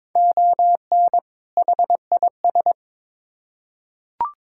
Morse code, heard as a keyed tone, about 22 wpm, 700 Hz. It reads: ON HIS
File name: Build A Phrase - 2-Word Phrases - 22wpm